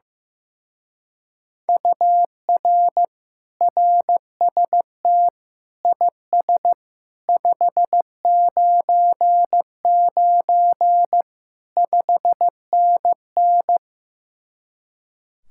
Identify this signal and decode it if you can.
Morse code, heard as a keyed tone, about 15 wpm, 700 Hz. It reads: UR RST IS 599 5NN
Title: Speed-Racing Common QSO Phrases Encore - 15wpm